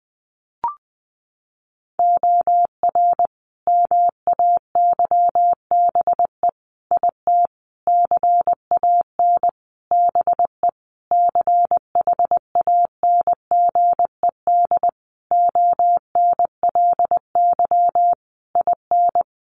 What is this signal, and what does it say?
Morse code, heard as a keyed tone, about 20 wpm, 700 Hz. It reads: OR MAYBE IT CAN BE CHANGED ONLY IN
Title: A Sound Of Thunder 20wpm